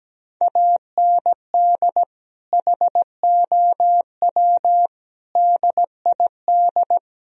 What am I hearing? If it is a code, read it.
Morse code, heard as a keyed tone, about 17 wpm, 700 Hz. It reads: AND HOW DID
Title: Sentences from Top 1000 Words - Review Part 4 - 17wpm